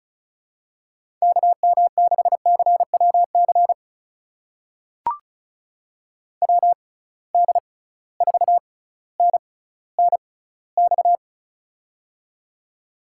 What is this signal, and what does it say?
Morse code, heard as a keyed tone, about 36 wpm, spaced as for 18 wpm, 700 Hz. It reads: KM6CWC WD4NNX
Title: USA Call Signs - 6 Characters - 6x Character Spacing - 35wpm